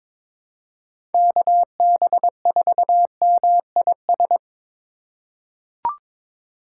Morse code, heard as a keyed tone, about 22 wpm, 700 Hz. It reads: KB4MIS
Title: USA Call Signs - 3x Character Spacing - 22wpm